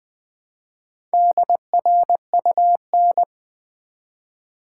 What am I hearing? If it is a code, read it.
Morse code, heard as a keyed tone, about 20 wpm, 700 Hz. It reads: DRUN
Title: Build A Word - 1 Letter At A Time - 3 to 5-Letter Words - 20wpm